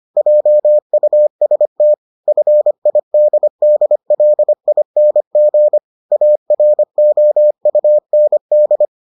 Morse code, heard as a keyed tone, about 25 wpm, 600 Hz. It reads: JUST FIDDLING AROUND